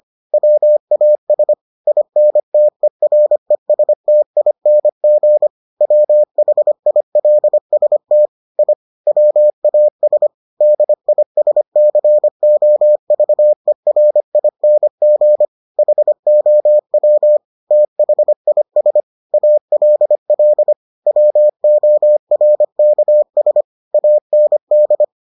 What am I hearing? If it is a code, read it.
Morse code, heard as a keyed tone, about 25 wpm, 600 Hz. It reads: WAS INTERESTING WHILST I WAS DISCOVERING HOW THIS ALL WORKS AND